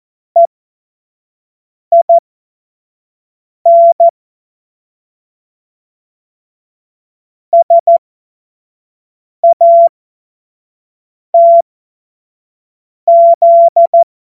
Morse code, heard as a keyed tone, about 14 wpm, spaced as for 5 wpm, 680 Hz. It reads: EIN SATZ
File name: Morsekurs / Witze / Witz 15. Juni 2021, 25 F70